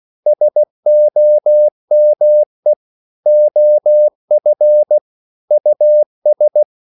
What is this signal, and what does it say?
Morse code, heard as a keyed tone, about 16 wpm, 600 Hz. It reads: SOME OF US
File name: CW / Practice / EnglishQuotations0116